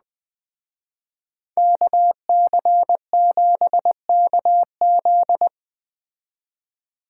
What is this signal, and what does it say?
Morse code, heard as a keyed tone, about 19 wpm, 700 Hz. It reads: KC7KZ
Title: Speed Racing - USA Call Signs 17wpm